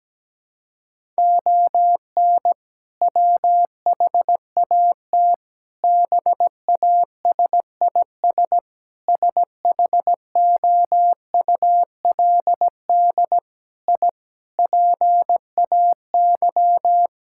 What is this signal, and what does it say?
Morse code, heard as a keyed tone, about 17 wpm, 700 Hz. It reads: ON WHAT BASIS SHOULD I PAY
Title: Sentences from Top 1000 Words - Review Part 5 - 17wpm